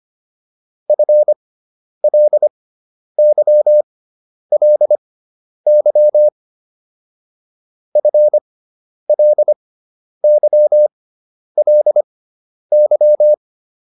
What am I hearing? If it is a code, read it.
Morse code, heard as a keyed tone, about 25 wpm, spaced as for 10 wpm, 600 Hz. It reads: FLYLY FLYLY